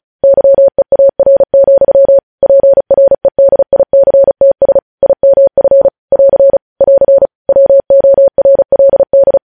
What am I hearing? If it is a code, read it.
Morse code, heard as a keyed tone, about 36 wpm, 550 Hz. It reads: YEAR, PREDICTS IMF <AR> <AR> WORLD
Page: news headlines in morse code at 35 wpm